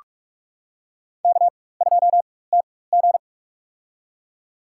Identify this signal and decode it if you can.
Morse code, heard as a keyed tone, about 46 wpm, spaced as for 21 wpm, 700 Hz. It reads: K2TG
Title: USA Call Signs - 4x Character Spacing - 45wpm